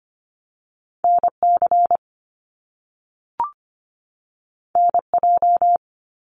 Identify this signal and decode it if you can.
Morse code, heard as a keyed tone, about 25 wpm, 700 Hz. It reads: NC NJ
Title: US States 25wpm